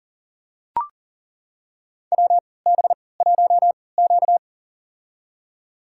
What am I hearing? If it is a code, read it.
Morse code, heard as a keyed tone, about 40 wpm, spaced as for 23 wpm, 700 Hz. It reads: WB1Q
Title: USA Call Signs - 3x Character Spacing - 40wpm